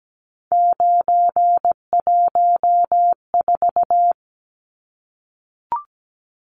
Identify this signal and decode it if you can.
Morse code, heard as a keyed tone, about 17 wpm, 700 Hz. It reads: 914
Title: Numbers - 3-Digit 17wpm